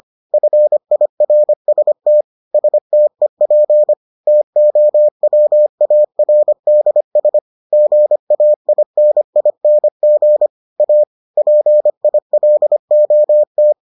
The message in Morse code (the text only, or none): FIRST STEP TOWARDS GAINING A PILOT